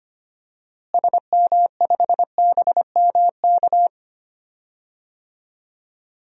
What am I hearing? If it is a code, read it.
Morse code, heard as a keyed tone, about 25 wpm, 700 Hz. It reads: SM5BMK
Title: DXCC-100 Call Signs - 6 Characters - Repeated 2x - 25wpm